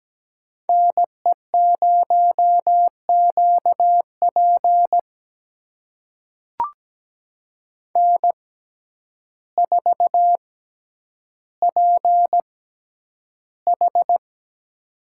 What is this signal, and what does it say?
Morse code, heard as a keyed tone, about 17 wpm, spaced as for 9 wpm, 700 Hz. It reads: NE0QP N4PH
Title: USA Call Signs - 6x Character Spacing - 17wpm